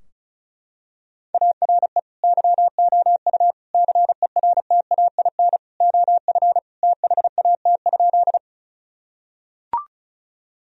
Morse code, heard as a keyed tone, about 36 wpm, 700 Hz. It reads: ARE YOU CERTAIN OF THAT?
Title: Sentences from Top 1000 Words - Review Part 2 - 35wpm